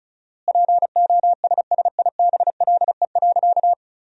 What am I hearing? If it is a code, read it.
Morse code, heard as a keyed tone, about 36 wpm, 700 Hz. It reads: POSSIBLE.
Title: 4x Word-Spacing - Sentences from Top 500 Words - 35wpm